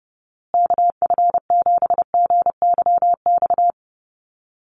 Morse code, heard as a keyed tone, about 30 wpm, 700 Hz. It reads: KF7GYX
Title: USA Call Signs Repeated 3x 30wpm